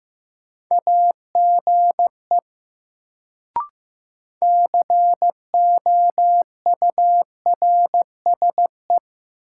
Morse code, heard as a keyed tone, about 15 wpm, 700 Hz. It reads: AGE COURSE
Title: Sentences from Top 500 Words - New Word Drill - 15wpm